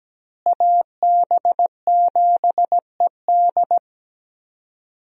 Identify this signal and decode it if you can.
Morse code, heard as a keyed tone, about 17 wpm, 700 Hz. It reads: AB7ED
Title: USA Call Signs - 5 Characters - Repeated 3x - 17wpm